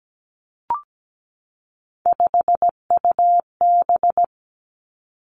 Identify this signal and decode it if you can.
Morse code, heard as a keyed tone, about 17 wpm, 700 Hz. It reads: 5UB
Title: Three Challenging Letter-Number Combinations 17wpm